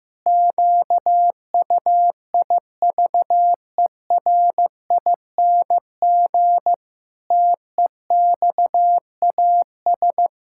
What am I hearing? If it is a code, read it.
Morse code, heard as a keyed tone, about 15 wpm, 700 Hz. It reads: QUIVERING TEXAS
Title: Pangram Challenge v2 - 15wpm